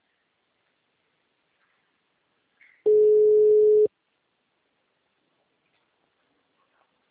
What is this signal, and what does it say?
Ringback tone (Europe, 425 Hz).